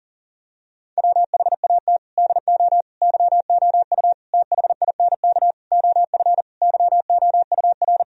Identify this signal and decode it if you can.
Morse code, heard as a keyed tone, about 40 wpm, 700 Hz. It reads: WHAT DO YOU THINK OF YOUR
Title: Sentences from Top 1000 Words - Review Part 4 - 40wpm